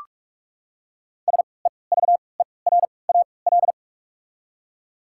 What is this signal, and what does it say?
Morse code, heard as a keyed tone, about 43 wpm, spaced as for 25 wpm, 700 Hz. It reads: SEVERAL
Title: 3x Character Spacing - Top 500 Words - 45wpm